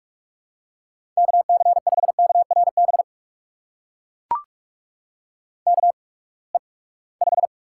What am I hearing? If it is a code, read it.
Morse code, heard as a keyed tone, about 44 wpm, spaced as for 25 wpm, 700 Hz. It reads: KC5KRB KE5
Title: USA Call Signs - 6 Characters - 8x Character Spacing - 45wpm